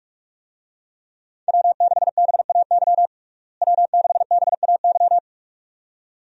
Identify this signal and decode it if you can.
Morse code, heard as a keyed tone, about 41 wpm, 700 Hz. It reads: W6BAY W6BAY
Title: USA Call Signs - 5 Characters - Repeated 2x - 45wpm